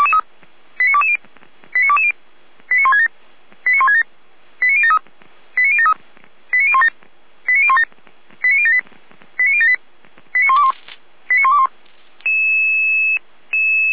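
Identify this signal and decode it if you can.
Five-tone selective call (ZVEI), heard as quick runs of five tones, tones of about 70 ms. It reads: 89220 89220 89167 89167 89073 89073 89017 89017 89078 89078 89121 89121
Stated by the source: ZVEI five-tone selective call 89168, received by SDR.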